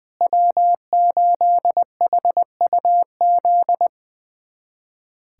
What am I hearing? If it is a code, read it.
Morse code, heard as a keyed tone, about 21 wpm, 700 Hz. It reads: W8HUZ
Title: USA Callsign DE Callsign - 5 Characters - 20wpm